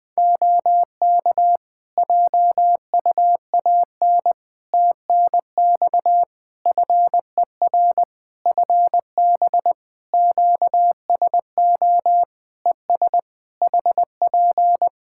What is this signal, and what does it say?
Morse code, heard as a keyed tone, about 20 wpm, 700 Hz. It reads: OK JUAN TNX FER FB QSO ES HP
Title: K1USN SST QSO Practice 20wpm